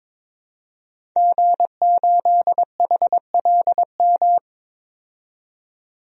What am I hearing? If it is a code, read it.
Morse code, heard as a keyed tone, about 22 wpm, 700 Hz. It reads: G8HLM
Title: DXCC-100 Call Signs - 5 Characters - 22wpm